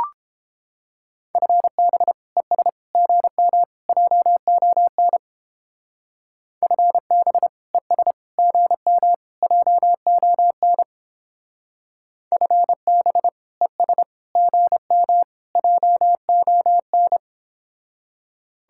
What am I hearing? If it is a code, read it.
Morse code, heard as a keyed tone, about 29 wpm, 700 Hz. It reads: FB ES GM JON FB ES GM JON FB ES GM JON